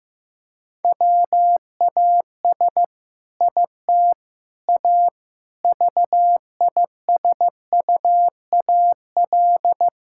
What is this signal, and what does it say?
Morse code, heard as a keyed tone, about 15 wpm, 700 Hz. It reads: WAS IT A VISUAL
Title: Sentences from Top 1000 Words - Review Part 6 - 15wpm